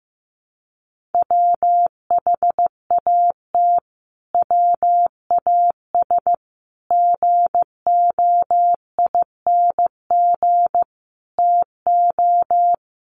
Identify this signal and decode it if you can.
Morse code, heard as a keyed tone, about 15 wpm, 700 Hz. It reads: WHAT WAS GOING TO
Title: Sets of 5 Words Encore 15wpm